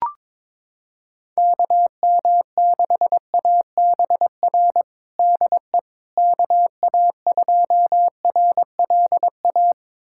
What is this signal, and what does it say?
Morse code, heard as a keyed tone, about 22 wpm, 700 Hz. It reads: KM6ABR DE KA2RLA
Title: USA Callsign DE Callsign 22wpm